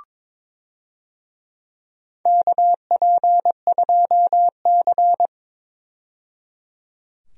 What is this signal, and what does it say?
Morse code, heard as a keyed tone, about 22 wpm, 700 Hz. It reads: KP2C
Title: DXCC-100 Call Signs - 4 Characters - 22wpm